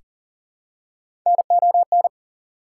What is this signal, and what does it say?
Morse code, heard as a keyed tone, about 40 wpm, 700 Hz. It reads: NON
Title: Prefixes and Suffixes 40wpm